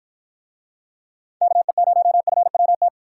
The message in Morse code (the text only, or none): KE0FWT